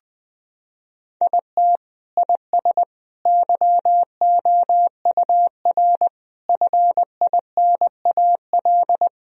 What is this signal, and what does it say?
Morse code, heard as a keyed tone, about 20 wpm, 700 Hz. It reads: IT IS YOUR FINAL
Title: Sentences from Top 1000 Words - Review Part 2 - 20wpm